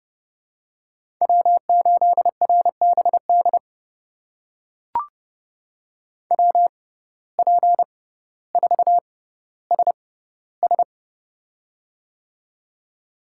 Morse code, heard as a keyed tone, about 30 wpm, spaced as for 15 wpm, 700 Hz. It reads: W8RBD WP4SS
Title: USA Call Signs - 5 Characters - 6x Character Spacing - 30wpm